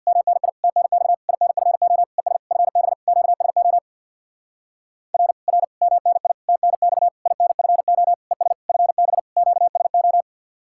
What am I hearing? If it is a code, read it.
Morse code, heard as a keyed tone, about 61 wpm, 700 Hz. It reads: MNI TNX INFO ES FB QSO R R MNI TNX INFO ES FB QSO
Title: Speed-Racing Common QSO Phrases - 50wpm